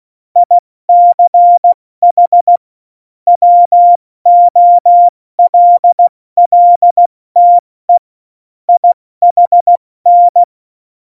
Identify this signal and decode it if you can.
Morse code, heard as a keyed tone, about 16 wpm, spaced as for 14 wpm, 700 Hz. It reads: ICH WOLLTE IHN
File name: Morsekurs / Witze / Witz 11 Juli 17, Tempo 70 F80